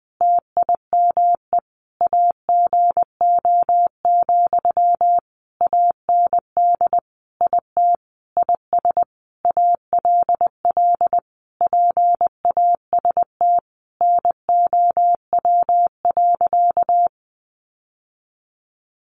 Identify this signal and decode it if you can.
Morse code, heard as a keyed tone, about 20 wpm, 700 Hz. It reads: TIME AGO, AND IT IS ALL PAST NOW.
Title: Sentences from Top 600 Words 20wpm